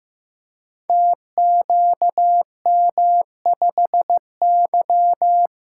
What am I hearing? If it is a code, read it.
Morse code, heard as a keyed tone, about 15 wpm, 700 Hz. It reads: TQM5Y
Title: Code Groups - 5 Letter:Numbers - 15wpm